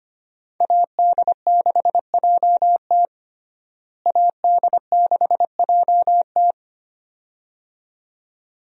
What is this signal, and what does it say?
Morse code, heard as a keyed tone, about 25 wpm, 690 Hz. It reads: AD6JT AD6JT
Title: USA Call Signs - 5 Characters - Repeated 2x - 25wpm